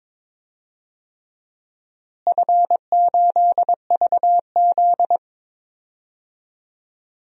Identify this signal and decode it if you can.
Morse code, heard as a keyed tone, about 22 wpm, 700 Hz. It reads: F8VZ
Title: DXCC-100 Call Signs - 4 Characters - Repeated 3x - 22wpm